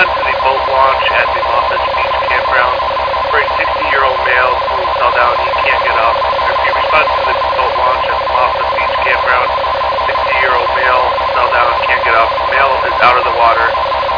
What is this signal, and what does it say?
Two-tone teleprinter signal, about 450 Hz shift, with speech.